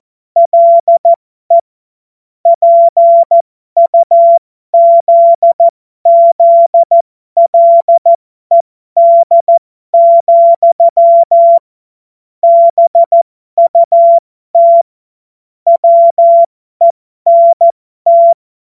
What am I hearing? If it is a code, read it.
Morse code, heard as a keyed tone, about 14 wpm, spaced as for 12 wpm, 680 Hz. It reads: LE PUZZLED, BUT WENT